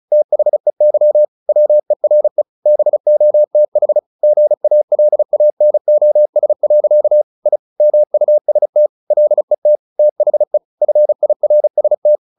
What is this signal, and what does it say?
Morse code, heard as a keyed tone, about 34 wpm, 600 Hz. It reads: THEY WERE BOTH GALANOS. I MUST LET THE FIRST
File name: the old man and the sea 2 35wpm